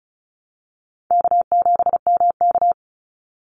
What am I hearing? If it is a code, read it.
Morse code, heard as a keyed tone, about 34 wpm, 700 Hz. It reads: K7MK
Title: NA SOTA Activator Callsigns 35wpm